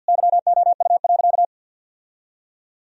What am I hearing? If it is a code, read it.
Morse code, heard as a keyed tone, about 50 wpm, 700 Hz. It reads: YOU.
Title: Sentences from Top 1000 Words - Review Part 2 - 50wpm